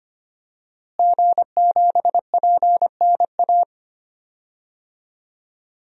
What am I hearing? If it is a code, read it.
Morse code, heard as a keyed tone, about 25 wpm, 700 Hz. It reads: G7PNA
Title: DXCC-100 Call Signs - 5 Characters - Repeated 3x - 25wpm